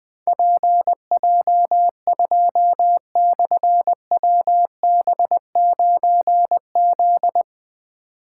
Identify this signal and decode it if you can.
Morse code, heard as a keyed tone, about 20 wpm, 700 Hz. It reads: PJ2/WB9Z
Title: Difficult Call Signs 20wpm